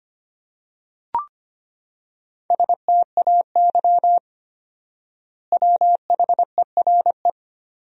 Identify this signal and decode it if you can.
Morse code, heard as a keyed tone, about 25 wpm, 700 Hz. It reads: STAY WHERE
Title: 4x Word-Spacing - Sentences from Top 500 Words - 25wpm